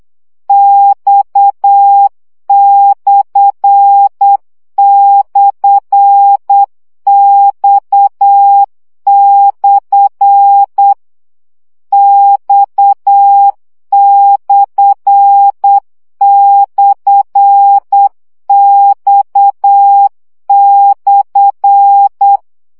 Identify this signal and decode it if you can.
Morse code, heard as a keyed tone, about 8 wpm, 800 Hz. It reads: X//X/ X//X/
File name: CW gyakorlas x per 7 baud